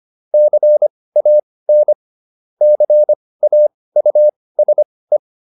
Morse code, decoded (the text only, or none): CAN CAUSE